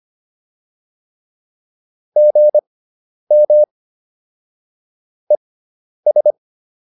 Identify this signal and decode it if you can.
Morse code, heard as a keyed tone, about 25 wpm, spaced as for 10 wpm, 600 Hz. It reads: GM ES